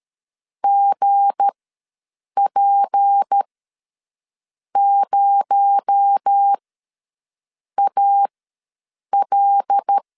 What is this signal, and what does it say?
Morse code, heard as a keyed tone, about 13 wpm, spaced as for 6 wpm, 800 Hz. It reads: GP0AL
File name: LEZIONI / 7 GF9Z / 7 GF9 ws 12wpm 3344